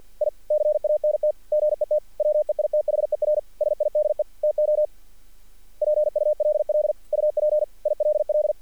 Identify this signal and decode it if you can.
Morse code, heard as a keyed tone, about 48 wpm, 600 Hz. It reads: I CANT GET WEITHER SIDE TO PULL UP ILL
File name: archived / CRSnet